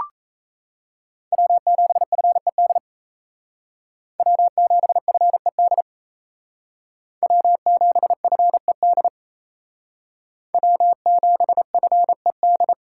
Morse code, heard as a keyed tone, about 36 wpm, 700 Hz. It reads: W7FED W7FED W7FED W7FED